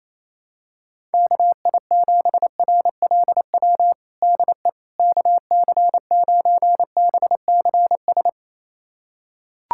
Morse code, heard as a keyed tone, about 27 wpm, 700 Hz. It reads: KI7RLW DE KC9BCS
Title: USA Callsign DE Callsign - 6 Characters - 28wpm